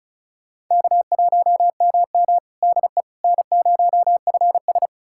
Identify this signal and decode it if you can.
Morse code, heard as a keyed tone, about 36 wpm, 700 Hz. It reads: K1MM DE N0FS
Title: USA Callsign DE Callsign - 4 Characters - 35wpm